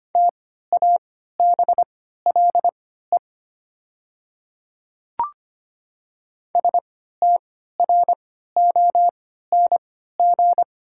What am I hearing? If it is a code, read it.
Morse code, heard as a keyed tone, about 25 wpm, spaced as for 14 wpm, 700 Hz. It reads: TABLE STRONG
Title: 3x Character Spacing - Top 500 Words - 25wpm